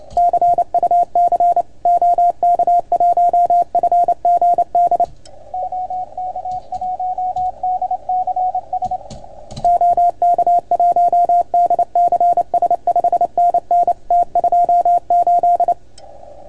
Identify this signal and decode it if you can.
Morse code, heard as a keyed tone, about 28 wpm, 690 Hz. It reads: CUC OK1FGD OK1DCS OK1DCS 5NN T28
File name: Archiv / CUC310 / OK1FGD CUC 310 pileup